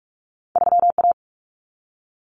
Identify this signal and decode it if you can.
Morse code, heard as a keyed tone, about 46 wpm, 700 Hz. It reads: 3A
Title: Common Cut Numbers - 2-Digit 45wpm